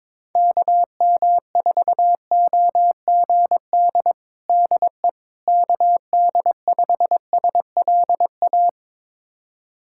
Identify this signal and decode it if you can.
Morse code, heard as a keyed tone, about 22 wpm, 700 Hz. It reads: KM4OGD DE KD5SLA
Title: USA Callsign DE Callsign - 6 Characters - 22wpm